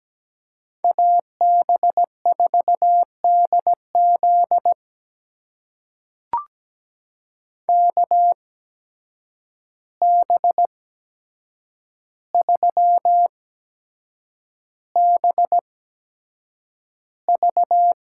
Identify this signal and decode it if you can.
Morse code, heard as a keyed tone, about 17 wpm, spaced as for 7 wpm, 700 Hz. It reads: AB4DZ KB3BV
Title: USA Call Signs - 5 Characters - 8x Character Spacing - 17wpm